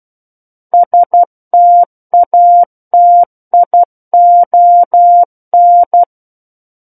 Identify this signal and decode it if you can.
Morse code, heard as a keyed tone, about 12 wpm, 700 Hz. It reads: STATION
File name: Book One / 03-Chapter 3 - 12 WPM